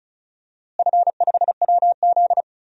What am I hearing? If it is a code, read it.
Morse code, heard as a keyed tone, about 34 wpm, 700 Hz. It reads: F5WZ
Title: DXCC-100 Call Signs - 4 Characters - Repeated 3x - 35wpm